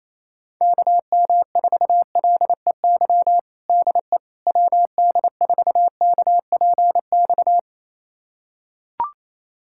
Morse code, heard as a keyed tone, about 28 wpm, 700 Hz. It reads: KM4LEY DE WD4KPX
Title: USA Callsign DE Callsign 28wpm